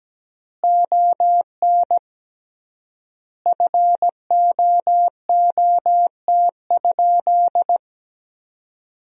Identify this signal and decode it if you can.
Morse code, heard as a keyed tone, about 17 wpm, 700 Hz. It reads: ON FOOT?
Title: 3x Word-Spacing - Sentences from Top 500 Words - 17wpm